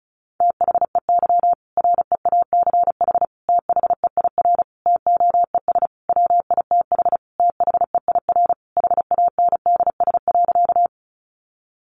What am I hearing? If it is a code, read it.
Morse code, heard as a keyed tone, about 34 wpm, 700 Hz. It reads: THEY REACH THEIR TOES WITH THEIR HANDS.
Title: Easy Phrases 35wpm